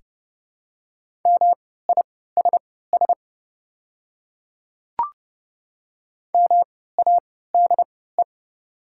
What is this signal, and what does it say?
Morse code, heard as a keyed tone, about 30 wpm, spaced as for 17 wpm, 690 Hz. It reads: MISS MADE
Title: 3x Character Spacing - Top 500 Words - 30wpm